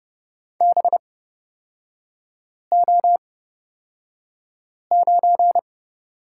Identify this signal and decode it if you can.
Morse code, heard as a keyed tone, about 29 wpm, spaced as for 5 wpm, 700 Hz. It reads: BO9
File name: Lesson 36e - BT - Three Characters 30wpm